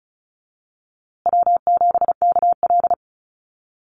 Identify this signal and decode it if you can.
Morse code, heard as a keyed tone, about 34 wpm, 700 Hz. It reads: W7KL
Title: USA Call Signs Repeated 2x 35wpm